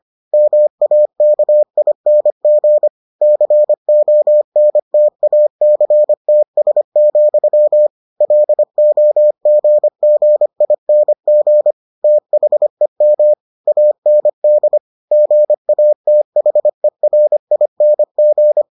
Morse code, heard as a keyed tone, about 25 wpm, 600 Hz. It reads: MAKING CONTACTS, LOGGING THEM AND GATHERING